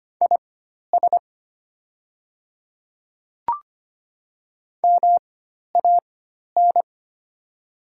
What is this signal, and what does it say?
Morse code, heard as a keyed tone, about 25 wpm, spaced as for 12 wpm, 700 Hz. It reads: IS MAN
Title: 4x Character Spacing - Top 300 Words - 25wpm